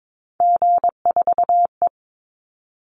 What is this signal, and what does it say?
Morse code, heard as a keyed tone, about 22 wpm, 700 Hz. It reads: G4E
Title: Call Signs - 22wpm